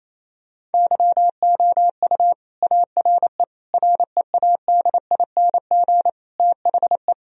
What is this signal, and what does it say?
Morse code, heard as a keyed tone, about 28 wpm, 700 Hz. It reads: YOU ARE READING THE
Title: Sentences from Top 1000 Words - Review Part 3 - 28wpm